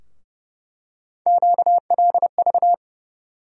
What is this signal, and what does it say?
Morse code, heard as a keyed tone, about 30 wpm, 700 Hz. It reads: QLV
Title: USA Call Sign Postfixes - 3 Letters 30wpm